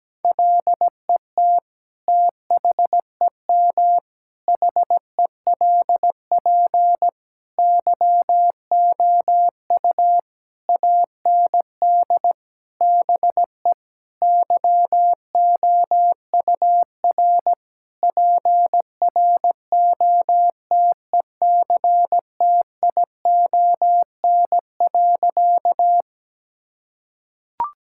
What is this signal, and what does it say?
Morse code, heard as a keyed tone, about 17 wpm, 700 Hz. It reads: LET THEM HELP YOU AND BE YOUR PROTECTION.
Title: Sentences from Top 1600 Words 17wpm